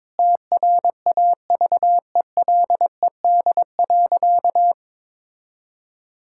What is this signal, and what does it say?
Morse code, heard as a keyed tone, about 22 wpm, 700 Hz. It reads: TRAVELED.
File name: practice / 22wpm / Sentences with 3-Word Phrases - 22wpm